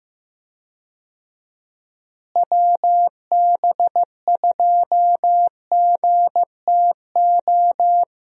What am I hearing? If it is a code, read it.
Morse code, heard as a keyed tone, about 15 wpm, 700 Hz. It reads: WB2GTO
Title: USA Call Signs - 6 Characters - Repeated 3x - 15wpm